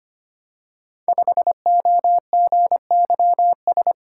Morse code, heard as a keyed tone, about 25 wpm, 700 Hz. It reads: 5OGYS
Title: Code Groups - 5 Letter:Numbers - 25wpm